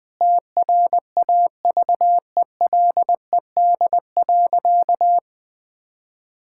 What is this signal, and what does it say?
Morse code, heard as a keyed tone, about 20 wpm, 700 Hz. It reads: TRAVELED.
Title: Sentences with 3-Word Phrases - Spoken Beginning - 20wpm